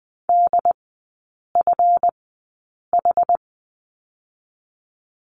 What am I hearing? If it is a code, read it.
Morse code, heard as a keyed tone, about 20 wpm, spaced as for 8 wpm, 690 Hz. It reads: DFH
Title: 145 Lesson 24e - Letter y - Three Characters 20wpm